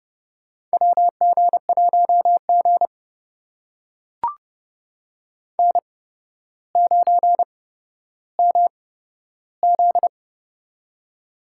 Morse code, heard as a keyed tone, about 30 wpm, spaced as for 13 wpm, 700 Hz. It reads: WG1G N9MZ
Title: USA Call Signs - 4 Characters - 8x Character Spacing - 30wpm